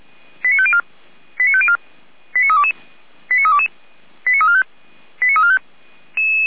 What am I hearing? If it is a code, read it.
Five-tone selective call (ZVEI), heard as quick runs of five tones, tones of about 70 ms. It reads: 89504 89504 89322 89322 89345 89345